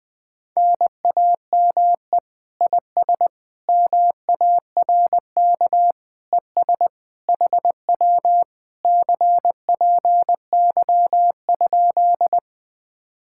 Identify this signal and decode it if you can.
Morse code, heard as a keyed tone, about 20 wpm, 700 Hz. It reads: NAME IS MARK ES HW CPY?
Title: Speed-Racing Common QSO Phrases - 17wpm